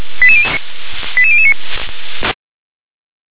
Five-tone selective call (ZVEI), heard as quick runs of five tones, tones of about 70 ms. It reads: 88009